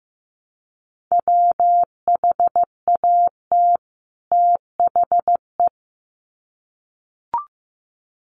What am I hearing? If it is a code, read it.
Morse code, heard as a keyed tone, about 15 wpm, 700 Hz. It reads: WHAT THE
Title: Sets of 2 Words 15wpm